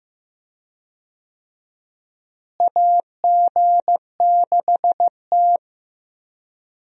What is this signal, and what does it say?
Morse code, heard as a keyed tone, about 15 wpm, 700 Hz. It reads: AG6T
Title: USA Call Signs - 4 Characters - Repeated 2x - 15wpm